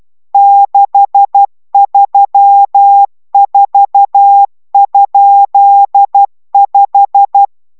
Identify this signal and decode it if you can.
Morse code, heard as a keyed tone, about 12 wpm, 800 Hz. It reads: 634?5